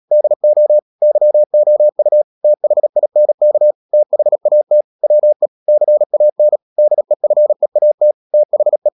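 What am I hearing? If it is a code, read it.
Morse code, heard as a keyed tone, about 37 wpm, 600 Hz. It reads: DO YOU THINK THAT WE CAN DEFEAT THE
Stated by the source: Morse code practice at 35–39 wpm sending English text of Ken Follett's novel Never, from CW med Gnister.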